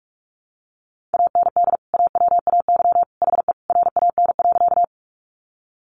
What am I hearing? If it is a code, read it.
Morse code, heard as a keyed tone, about 44 wpm, 700 Hz. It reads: AND AWAY HE RAN.
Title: Sentences from Top 600 Words 45wpm